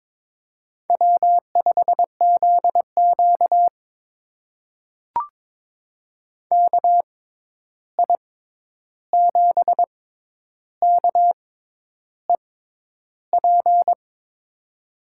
Morse code, heard as a keyed tone, about 22 wpm, spaced as for 10 wpm, 700 Hz. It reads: W5ZQ KI7KEP
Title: USA Call Signs - 6x Character Spacing - 22wpm